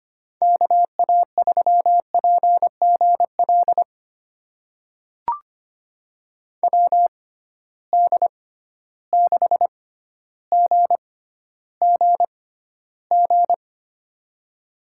Morse code, heard as a keyed tone, about 25 wpm, spaced as for 13 wpm, 700 Hz. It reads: KA3PGL WD6GGG
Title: USA Call Signs - 6 Characters - 6x Character Spacing - 25wpm